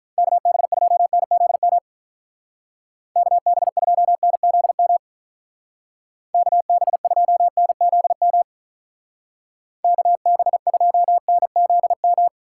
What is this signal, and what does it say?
Morse code, heard as a keyed tone, about 43 wpm, 700 Hz. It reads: KB2NZM KB2NZM KB2NZM KB2NZM
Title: Speed Racing - USA 2x3 Call Signs 35wpm